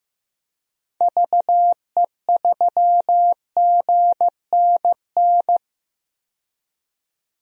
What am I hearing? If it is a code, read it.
Morse code, heard as a keyed tone, about 15 wpm, 700 Hz. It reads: VE3GNN
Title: DXCC-100 Call Signs - 6 Characters - 15wpm